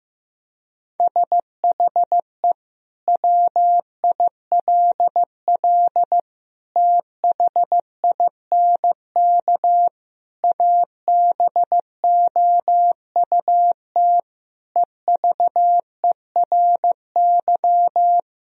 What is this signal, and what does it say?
Morse code, heard as a keyed tone, about 15 wpm, 700 Hz. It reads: SHE WILL THINK ABOUT EVERY
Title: Sentences from Top 1000 Words - Review Part 1 - 15wpm